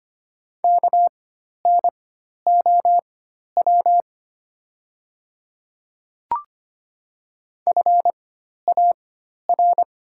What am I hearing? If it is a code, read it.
Morse code, heard as a keyed tone, about 25 wpm, spaced as for 12 wpm, 700 Hz. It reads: KNOW FAR
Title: 4x Character Spacing - Top 300 Words - 25wpm